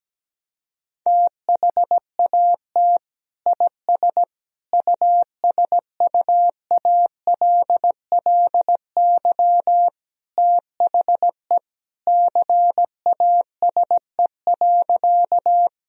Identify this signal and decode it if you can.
Morse code, heard as a keyed tone, about 17 wpm, 700 Hz. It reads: THAT IS USUALLY THE CASE.
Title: Sentences from Top 1000 Words - Review Part 3 - 17wpm